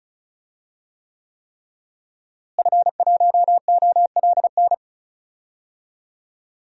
Morse code, heard as a keyed tone, about 34 wpm, 700 Hz. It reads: F1OLN
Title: DXCC-100 Call Signs - 5 Characters - Repeated 3x - 35wpm